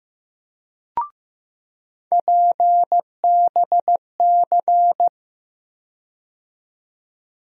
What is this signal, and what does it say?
Morse code, heard as a keyed tone, about 15 wpm, 700 Hz. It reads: PBC
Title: USA Call Sign Postfixes - 3 Letters 15wpm